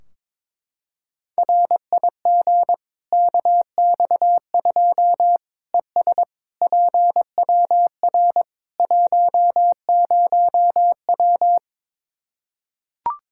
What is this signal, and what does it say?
Morse code, heard as a keyed tone, about 22 wpm, 700 Hz. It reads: RIG KX2 ES PWR 10W
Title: K1USN SST QSO Practice 22wpm